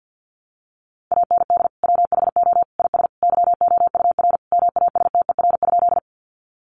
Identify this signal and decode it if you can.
Morse code, heard as a keyed tone, about 49 wpm, 700 Hz. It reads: AND WHO IS YOUR MASTER?
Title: Sentences from Top 1300 Words 50wpm